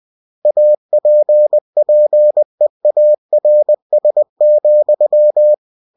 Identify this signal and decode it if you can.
Morse code, heard as a keyed tone, about 20 wpm, 600 Hz. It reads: APPEARS,
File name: cw-books / winnie-the-pooh / winnie-20wpm-page-0115